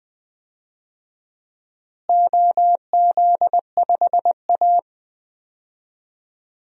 Morse code, heard as a keyed tone, about 20 wpm, 700 Hz. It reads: OZ5A
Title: DXCC-100 Call Signs - 4 Characters - Repeated 3x - 20wpm